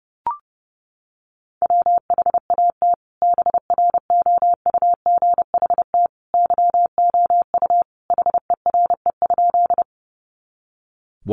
Morse code, heard as a keyed tone, about 30 wpm, 700 Hz. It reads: WHAT BROUGHT YOU HERE?
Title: Sentences from Top 500 Words 30wpm